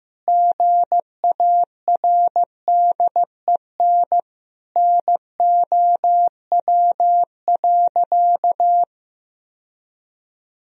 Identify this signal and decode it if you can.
Morse code, heard as a keyed tone, about 15 wpm, 700 Hz. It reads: GARDEN NOW.
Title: Sentences from Top 1000 Words - Review Part 4 - 15wpm